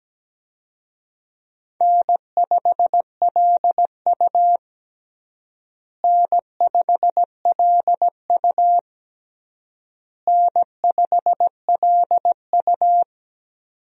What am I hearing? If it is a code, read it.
Morse code, heard as a keyed tone, about 17 wpm, 700 Hz. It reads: N5LU N5LU N5LU
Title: USA Call Signs - 4 Characters - Repeated 3x - 17wpm